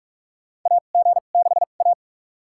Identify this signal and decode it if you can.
Morse code, heard as a keyed tone, about 44 wpm, spaced as for 32 wpm, 700 Hz. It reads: AG6A
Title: USA Call Signs - 4 Characters - 2x Character Spacing - 45wpm